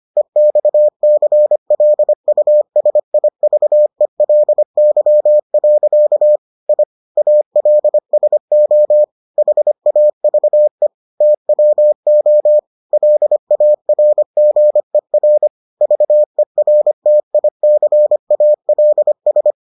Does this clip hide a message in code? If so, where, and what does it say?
Yes: Morse code from 0.2 to 19.5 s, heard as a keyed tone, about 25 wpm, 600 Hz. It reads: EXCLUSIVELY. I ALSO HAVE TWO LARGER VERTICALS